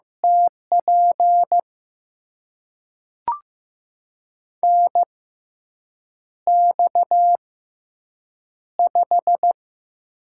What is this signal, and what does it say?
Morse code, heard as a keyed tone, about 15 wpm, spaced as for 7 wpm, 700 Hz. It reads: TP NX5
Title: USA Call Signs - 4 Characters - 6x Character Spacing - 15wpm